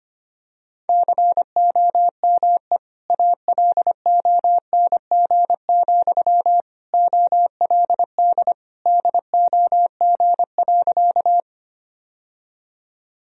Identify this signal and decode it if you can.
Morse code, heard as a keyed tone, about 25 wpm, 700 Hz. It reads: COME ALONG, OLD DOG.
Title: Sentences from Top 1000 Words - Review Part 2 - 25wpm